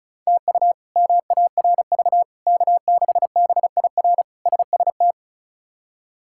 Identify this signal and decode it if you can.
Morse code, heard as a keyed tone, about 34 wpm, 700 Hz. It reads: TU MARV K6BIR SST
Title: K1USN SST QSO Practice - 35wpm